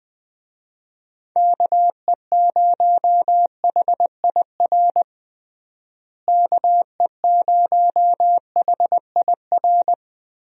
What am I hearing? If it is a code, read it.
Morse code, heard as a keyed tone, about 20 wpm, 700 Hz. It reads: KE0HIR KE0HIR
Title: USA Call Signs - 6 Characters - Repeated 3x - 20wpm